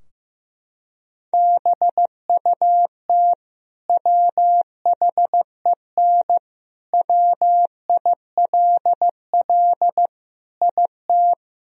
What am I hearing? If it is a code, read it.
Morse code, heard as a keyed tone, about 15 wpm, 700 Hz. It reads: BUT WHEN WILL IT
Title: Sentences from Top 1000 Words - Review Part 3 - 15wpm